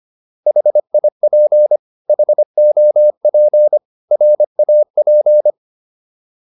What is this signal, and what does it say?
Morse code, heard as a keyed tone, about 25 wpm, 600 Hz. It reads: HIP HOP RAP